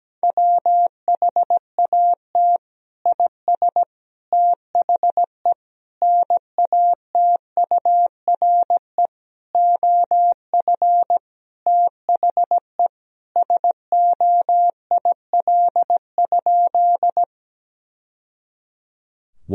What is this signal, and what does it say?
Morse code, heard as a keyed tone, about 17 wpm, 700 Hz. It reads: WHAT IS THE NATURE OF THE SOIL?
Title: Sentences from Top 1000 Words - Review Part 4 - 17wpm